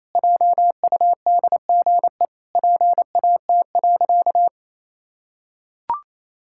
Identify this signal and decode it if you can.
Morse code, heard as a keyed tone, about 27 wpm, 700 Hz. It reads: JUDGE PAT.
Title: Pangram Challenge v2 - 28wpm